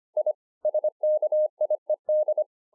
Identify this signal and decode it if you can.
Morse code, heard as a keyed tone, about 25 wpm, 610 Hz. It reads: I SKIED